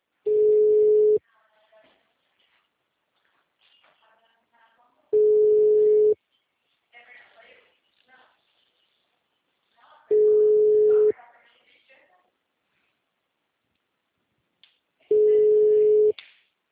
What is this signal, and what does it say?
Ringback tone (Europe, 425 Hz), with speech.